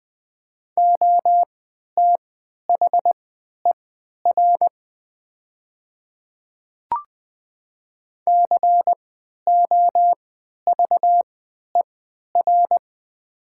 Morse code, heard as a keyed tone, about 20 wpm, spaced as for 11 wpm, 700 Hz. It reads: OTHER COVER
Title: 3x Character Spacing - Top 300 Words - 20wpm